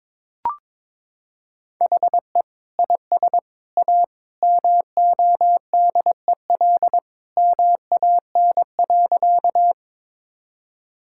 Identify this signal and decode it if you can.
Morse code, heard as a keyed tone, about 22 wpm, 700 Hz. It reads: HE IS A MODEL MAN.
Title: Sentences from Top 1000 Words - Review Part 5 - 22wpm